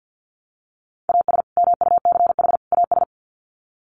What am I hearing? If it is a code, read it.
Morse code, heard as a keyed tone, about 50 wpm, 700 Hz. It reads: AS MUCH AS
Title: Sets of 3 Words 50wpm